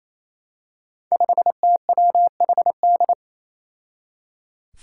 Morse code, heard as a keyed tone, about 28 wpm, 700 Hz. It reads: 5TWHD
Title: Five Challenging Letter-Number Combinations 28wpm